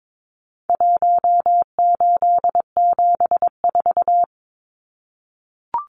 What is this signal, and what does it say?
Morse code, heard as a keyed tone, about 22 wpm, 700 Hz. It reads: 1874
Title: Numbers - Contest Style - 4-Digit 22wpm